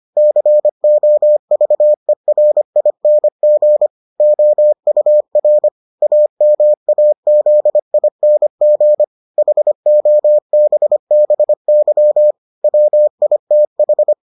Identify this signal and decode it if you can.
Morse code, heard as a keyed tone, about 25 wpm, 600 Hz. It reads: COVERING OUR AMAZING HOBBY WITH